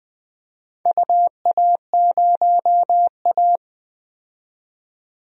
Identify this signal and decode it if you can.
Morse code, heard as a keyed tone, about 20 wpm, 700 Hz. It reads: UA0A
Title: DXCC-100 Call Signs - 4 Characters - 20wpm